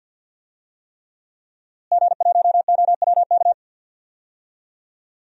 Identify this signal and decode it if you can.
Morse code, heard as a keyed tone, about 47 wpm, 700 Hz. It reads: G1OWK
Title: DXCC-100 Call Signs - 5 Characters - Repeated 3x - 50wpm